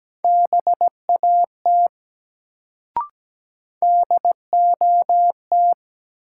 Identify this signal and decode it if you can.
Morse code, heard as a keyed tone, about 17 wpm, 700 Hz. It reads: BAT DOT
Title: Sentences from Top 1500 Words - New Word Drill - 17wpm